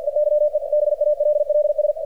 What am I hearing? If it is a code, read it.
Morse code, heard as a keyed tone, about 48 wpm, 600 Hz, with radio noise. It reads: EYE BALLS
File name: archived / CRSnet